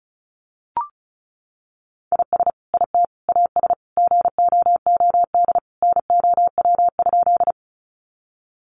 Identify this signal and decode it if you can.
Morse code, heard as a keyed tone, about 34 wpm, 700 Hz. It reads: IS IT AS GOOD NOW?
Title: Sentences from Top 200 Words 35wpm